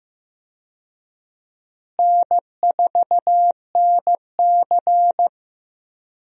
Morse code, heard as a keyed tone, about 15 wpm, 700 Hz. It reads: N4NC
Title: USA Call Signs - 4 Characters - Repeated 3x - 15wpm